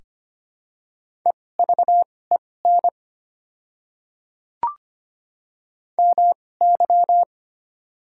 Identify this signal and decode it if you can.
Morse code, heard as a keyed tone, about 25 wpm, spaced as for 18 wpm, 700 Hz. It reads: EVEN MY
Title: 2x Character Spacing - Top 200 Words - 25wpm